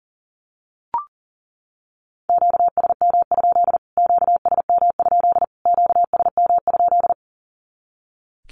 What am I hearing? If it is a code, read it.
Morse code, heard as a keyed tone, about 40 wpm, 700 Hz. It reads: QSM? QSM? QSM?
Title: Q-Codes Repeated 3x 40wpm